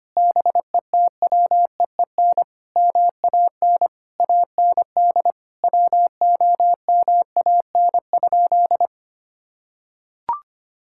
Morse code, heard as a keyed tone, about 25 wpm, 700 Hz. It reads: BETWEEN MAN AND WOMAN?
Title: Sentences from Top 1600 Words 25wpm